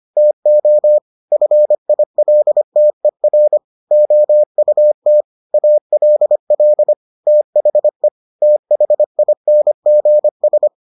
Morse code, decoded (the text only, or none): TO FILTER OUT ALL THE THINGS